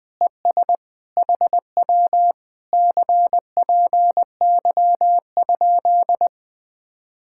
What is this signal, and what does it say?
Morse code, heard as a keyed tone, about 20 wpm, 700 Hz. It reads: ES HW CPY?